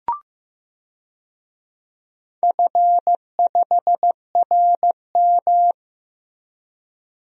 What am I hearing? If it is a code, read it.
Morse code, heard as a keyed tone, about 15 wpm, 700 Hz. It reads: F5RM
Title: DXCC-100 Call Signs - 4 Characters - Repeated 3x - 15wpm